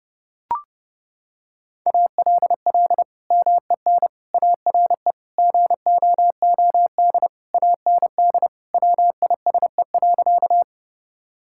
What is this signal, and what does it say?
Morse code, heard as a keyed tone, about 30 wpm, 700 Hz. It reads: ALL MEN ARE GOOD AND WISE.
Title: Sentences from Top 1600 Words 30wpm